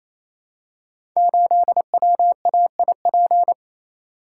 Morse code, heard as a keyed tone, about 27 wpm, 700 Hz. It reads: 8WAIP
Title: Code Groups - 5 Letter:Numbers - 28wpm